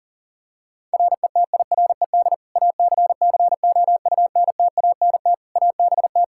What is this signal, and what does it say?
Morse code, heard as a keyed tone, about 40 wpm, 700 Hz. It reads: RETIRED ACCOUNTANT ABT
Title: Practice QSO - Exchange 3 - More Info - 40wpm